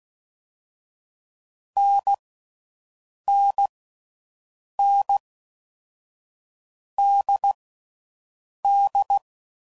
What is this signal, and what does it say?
Morse code, heard as a keyed tone, about 16 wpm, 800 Hz. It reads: N N N D D
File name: Combinations / Difficult Combos / NDB NDB6